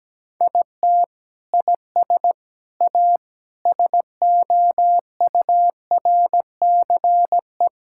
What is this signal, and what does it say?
Morse code, heard as a keyed tone, about 17 wpm, 700 Hz. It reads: IT IS A SOURCE
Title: Sentences from Top 1000 Words - Review Part 5 - 17wpm